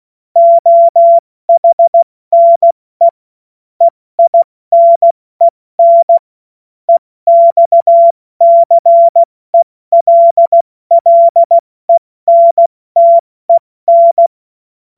Morse code, heard as a keyed tone, about 16 wpm, 680 Hz. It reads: OHNE EINEN EXCELLENTEN
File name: Morsekurs / Witze / Witz 21.November 2023 70 F80